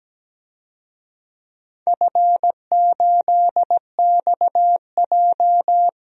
Morse code, heard as a keyed tone, about 17 wpm, 700 Hz. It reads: F8XJ
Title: DXCC-100 Call Signs - 4 Characters - 17wpm